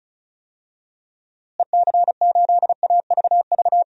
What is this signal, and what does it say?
Morse code, heard as a keyed tone, about 34 wpm, 700 Hz. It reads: EC8AVV
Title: DXCC-100 Call Signs - 6 Characters - Repeated 3x - 35wpm